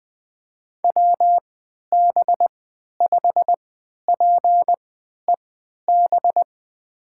Morse code, heard as a keyed tone, about 20 wpm, spaced as for 11 wpm, 700 Hz. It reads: WB5PEB
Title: USA Call Signs - 6 Characters - 3x Character Spacing - 20wpm